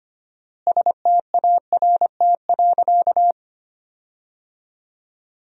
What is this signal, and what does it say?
Morse code, heard as a keyed tone, about 25 wpm, 700 Hz. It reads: START.
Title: 4x Word-Spacing - Sentences from Top 300 Words - 25wpm